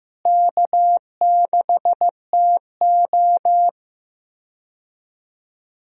Morse code, heard as a keyed tone, about 15 wpm, 700 Hz. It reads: K6TO
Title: USA Call Signs - 4 Characters - Repeated 2x - 15wpm